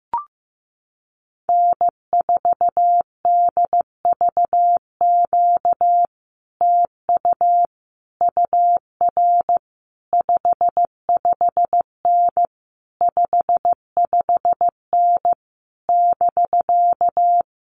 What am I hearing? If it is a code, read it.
Morse code, heard as a keyed tone, about 15 wpm, 700 Hz. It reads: N4DVQ TU UR 55N 55N <BK>
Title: Common QSO Phrases Encore 15wpm